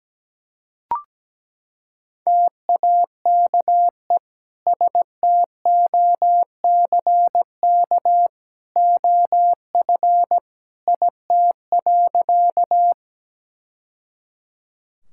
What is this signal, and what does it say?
Morse code, heard as a keyed tone, about 17 wpm, 700 Hz. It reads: TAKE STOCK OF IT.
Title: Sentences from Top 1000 Words - Review Part 5 - 17wpm